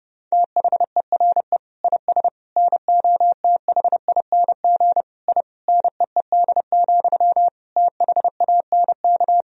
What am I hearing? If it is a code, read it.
Morse code, heard as a keyed tone, about 32 wpm, 700 Hz. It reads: THERE IS NOTHING I NEED, THANK
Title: Sentences from Top 1000 Words - Review Part 1 - 30wpm